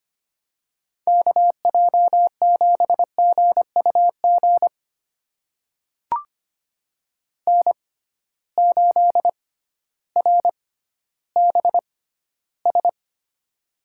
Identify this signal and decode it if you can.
Morse code, heard as a keyed tone, about 25 wpm, spaced as for 13 wpm, 700 Hz. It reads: KJ7GUG N8RBS
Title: USA Call Signs - 6x Character Spacing - 25wpm